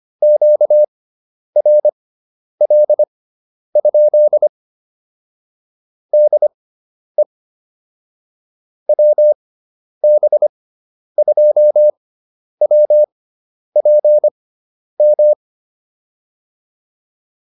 Morse code, heard as a keyed tone, about 25 wpm, spaced as for 10 wpm, 600 Hz. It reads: QRL? DE WB2WPM